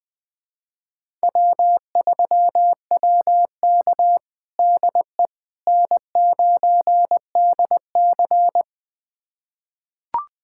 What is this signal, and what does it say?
Morse code, heard as a keyed tone, about 20 wpm, 700 Hz. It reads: W3WK DE N9DC
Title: USA Callsign DE Callsign - 4 Characters - 20wpm